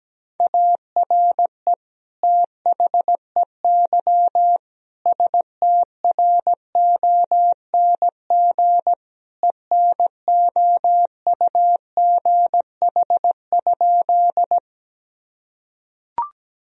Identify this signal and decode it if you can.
Morse code, heard as a keyed tone, about 17 wpm, 700 Hz. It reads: ARE THEY STRONG ENOUGH?
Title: Sentences from Top 1000 Words - Review Part 3 - 17wpm